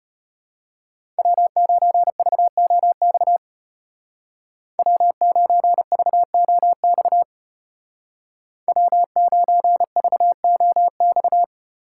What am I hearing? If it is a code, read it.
Morse code, heard as a keyed tone, about 31 wpm, 700 Hz. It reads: W9VOX W9VOX W9VOX
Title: Speed Racing - USA 1x3 Call Signs 25wpm